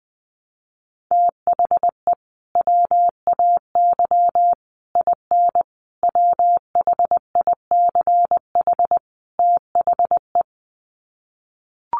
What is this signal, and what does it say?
Morse code, heard as a keyed tone, about 20 wpm, 700 Hz. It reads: THE WAY IN WHICH THE
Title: Sets of 5 Words Encore2 20wpm